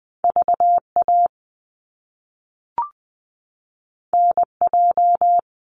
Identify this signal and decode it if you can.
Morse code, heard as a keyed tone, about 20 wpm, 700 Hz. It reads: VA NJ
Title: US States 20wpm